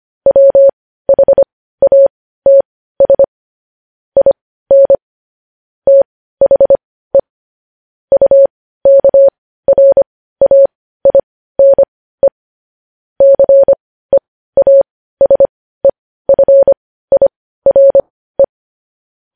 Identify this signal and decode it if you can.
Morse code, heard as a keyed tone, about 25 wpm, spaced as for 15 wpm, 550 Hz. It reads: WHATS IN THE UKRAINE CEASEFIRE
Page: news headlines in morse code at 15 wpm